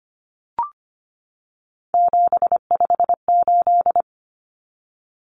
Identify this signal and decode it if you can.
Morse code, heard as a keyed tone, about 25 wpm, 700 Hz. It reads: 758
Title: Common Cut Numbers - 3-Digit 25wpm